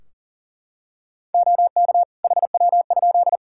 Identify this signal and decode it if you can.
Morse code, heard as a keyed tone, about 40 wpm, 700 Hz. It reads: OK HW?
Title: K1USN SST QSO Practice 40wpm